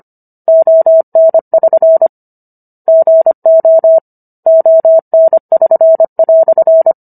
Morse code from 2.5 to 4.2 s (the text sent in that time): GO